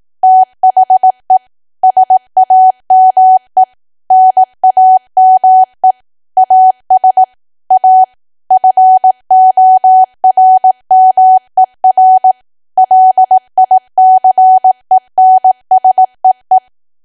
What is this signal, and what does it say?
Morse code, heard as a keyed tone, about 18 wpm, 750 Hz. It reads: THE SAME NAME AS A FORMER LICENSEE
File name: cwa / W1AW-Level2 / 18WPM